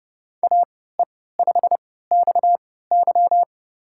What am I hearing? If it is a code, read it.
Morse code, heard as a keyed tone, about 30 wpm, spaced as for 17 wpm, 700 Hz. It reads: AE5XY
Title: USA Call Signs - 5 Characters - 3x Character Spacing - 30wpm